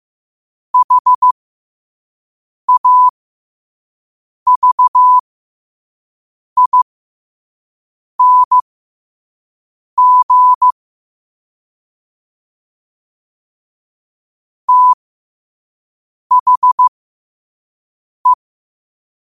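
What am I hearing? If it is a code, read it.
Morse code, heard as a keyed tone, about 15 wpm, spaced as for 5 wpm, 1000 Hz. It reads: HAVING THE
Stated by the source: 5 wpm Morse.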